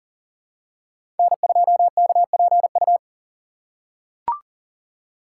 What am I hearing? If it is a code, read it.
Morse code, heard as a keyed tone, about 40 wpm, 700 Hz. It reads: N2KPU
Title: USA Call Signs - 5 Characters - 6x Character Spacing - 40wpm